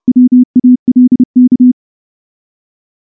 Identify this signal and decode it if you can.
Morse code, heard as a keyed tone, about 28 wpm, 250 Hz. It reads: WALK
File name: morse-training-30WPM